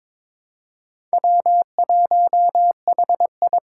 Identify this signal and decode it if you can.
Morse code, heard as a keyed tone, about 23 wpm, 700 Hz. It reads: W1HI
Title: Four Challenging Letter-Number Combinations 22wpm